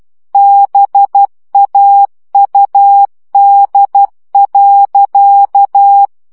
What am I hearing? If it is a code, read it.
Morse code, heard as a keyed tone, about 12 wpm, 800 Hz. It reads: BAUD.
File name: CW 10 baud